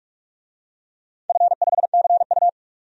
Morse code, heard as a keyed tone, about 46 wpm, 700 Hz. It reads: F5CU
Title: DXCC-100 Call Signs - 4 Characters - 45wpm